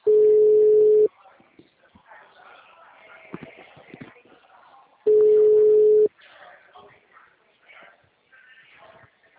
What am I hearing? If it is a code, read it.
Ringback tone (Europe, 425 Hz), with speech.
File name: out- -2008- - - .363